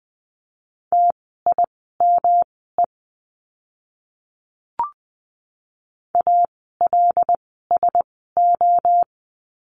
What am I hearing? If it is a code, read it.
Morse code, heard as a keyed tone, about 20 wpm, spaced as for 15 wpm, 700 Hz. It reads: TIME ALSO
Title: 160 Lesson 26f - Letter g - Words 20wpm